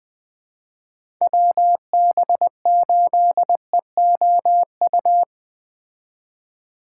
Morse code, heard as a keyed tone, about 20 wpm, 700 Hz. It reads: WB8EOU